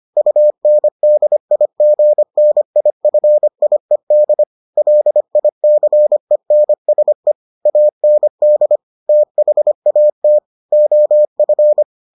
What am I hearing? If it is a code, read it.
Morse code, heard as a keyed tone, about 25 wpm, 600 Hz. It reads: UNDIGNIFIED LICENSE AND THAT OF